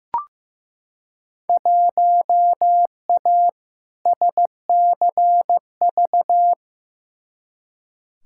Morse code, heard as a keyed tone, about 15 wpm, 700 Hz. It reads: 1A SCV
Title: Field Day Exchanges - 15wpm